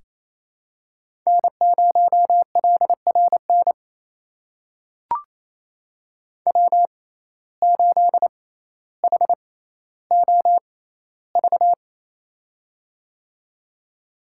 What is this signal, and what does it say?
Morse code, heard as a keyed tone, about 28 wpm, spaced as for 14 wpm, 700 Hz. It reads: N0LRN W8HOV
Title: USA Call Signs - 5 Characters - 6x Character Spacing - 28wpm